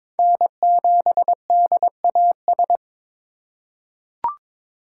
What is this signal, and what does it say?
Morse code, heard as a keyed tone, about 22 wpm, 700 Hz. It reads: N7DAS
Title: USA Call Signs - 5 Characters - 3x Character Spacing - 22wpm